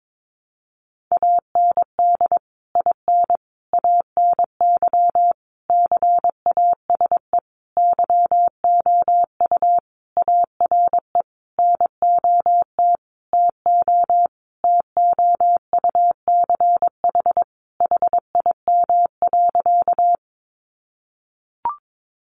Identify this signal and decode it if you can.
Morse code, heard as a keyed tone, about 22 wpm, 700 Hz. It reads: AND IN ANY CASE YOU ARE NOT TO TOUCH HIM.
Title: Sentences from Top 1200 Words 22wpm